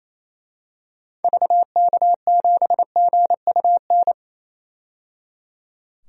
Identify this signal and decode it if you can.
Morse code, heard as a keyed tone, about 28 wpm, 700 Hz. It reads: VK7GUN
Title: DXCC-100 Call Signs - 6 Characters - 28wpm